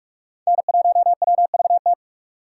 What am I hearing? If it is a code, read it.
Morse code, heard as a keyed tone, about 42 wpm, 700 Hz. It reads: N1WVT